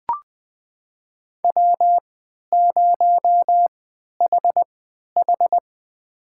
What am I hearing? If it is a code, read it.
Morse code, heard as a keyed tone, about 20 wpm, spaced as for 11 wpm, 700 Hz. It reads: W0HH
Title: USA Call Signs - 4 Characters - 3x Character Spacing - 20wpm